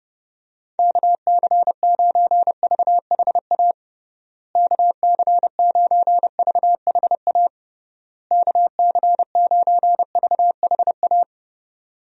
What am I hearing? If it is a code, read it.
Morse code, heard as a keyed tone, about 32 wpm, 700 Hz. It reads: KC9VHA KC9VHA KC9VHA
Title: USA Call Signs - 6 Characters - Repeated 3x - 30wpm